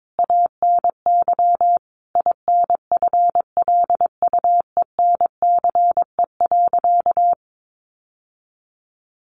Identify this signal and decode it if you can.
Morse code, heard as a keyed tone, about 22 wpm, 700 Hz. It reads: ANY INFLUENCE.
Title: Sentences from Top 1100 Words 22wpm